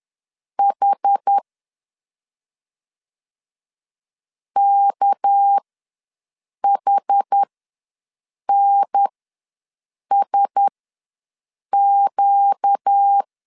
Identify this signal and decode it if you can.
Morse code, heard as a keyed tone, about 10 wpm, spaced as for 6 wpm, 800 Hz. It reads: H KHNSQ
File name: LEZIONI / 6 PKX0 / 6 PK ws 10wpm 3745